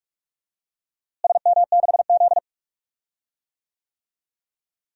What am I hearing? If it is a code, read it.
Morse code, heard as a keyed tone, about 45 wpm, 700 Hz. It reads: SM6Z
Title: DXCC-100 Call Signs - 4 Characters - Repeated 2x - 45wpm